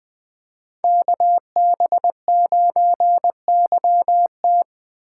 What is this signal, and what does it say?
Morse code, heard as a keyed tone, about 21 wpm, 700 Hz. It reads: KB9YT